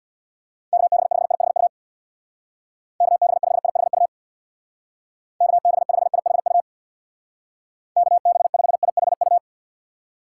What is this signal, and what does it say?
Morse code, heard as a keyed tone, about 61 wpm, 700 Hz. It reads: KB5IHU KB5IHU KB5IHU KB5IHU
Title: Speed Racing - USA Call Signs 50wpm